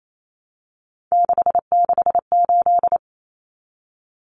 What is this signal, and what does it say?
Morse code, heard as a keyed tone, about 28 wpm, 700 Hz. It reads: 668
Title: Numbers - 3-Digit 28wpm